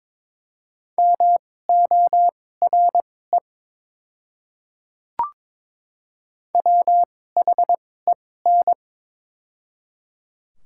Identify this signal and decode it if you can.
Morse code, heard as a keyed tone, about 22 wpm, spaced as for 16 wpm, 700 Hz. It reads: MORE WHEN
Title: 2x Character Spacing - Top 200 Words - 22wpm